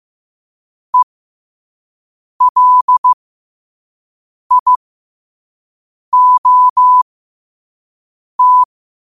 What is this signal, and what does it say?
Morse code, heard as a keyed tone, about 15 wpm, spaced as for 5 wpm, 1000 Hz. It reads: ELIOT
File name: qotdfiles / 5wpm